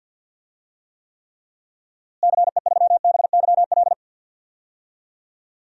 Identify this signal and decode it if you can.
Morse code, heard as a keyed tone, about 52 wpm, 700 Hz. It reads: CE3BYL
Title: DXCC-100 Call Signs - 6 Characters - Repeated 3x - 50wpm